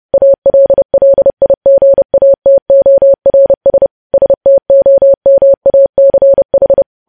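Morse code, heard as a keyed tone, about 30 wpm, 550 Hz. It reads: ALLIGATORS STOMACH
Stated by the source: News headlines sent as Morse practice at 30.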